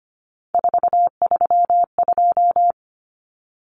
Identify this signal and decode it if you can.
Morse code, heard as a keyed tone, about 25 wpm, 700 Hz. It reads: 432
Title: Common Cut Numbers - 3-Digit 25wpm